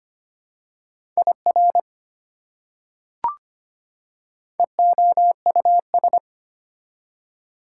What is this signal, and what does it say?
Morse code, heard as a keyed tone, about 25 wpm, 700 Hz. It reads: IR EOUS
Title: Prefixes and Suffixes 25wpm